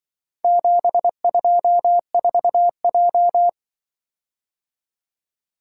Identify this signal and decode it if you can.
Morse code, heard as a keyed tone, about 24 wpm, 700 Hz. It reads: 724J
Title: Speed-Racing - Four Letter-Number Combinations - 20wpm